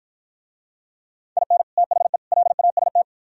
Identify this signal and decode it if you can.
Morse code, heard as a keyed tone, about 50 wpm, 700 Hz. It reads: IN THE LAST